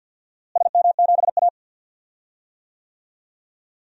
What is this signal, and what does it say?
Morse code, heard as a keyed tone, about 52 wpm, 700 Hz. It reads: SM7A
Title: DXCC-100 Call Signs - 4 Characters - Repeated 3x - 50wpm